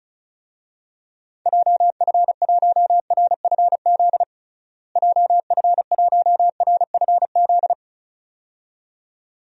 Morse code, heard as a keyed tone, about 36 wpm, 700 Hz. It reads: JF1RFZ JF1RFZ
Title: DXCC-100 Call Signs - 6 Characters - Repeated 2x - 35wpm